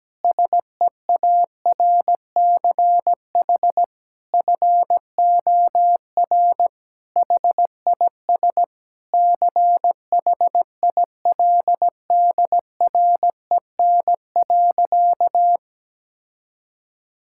Morse code, heard as a keyed tone, about 17 wpm, 700 Hz. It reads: SEARCH FOR HIS CHILDREN.
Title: Sentences from Top 1000 Words - Review Part 5 - 17wpm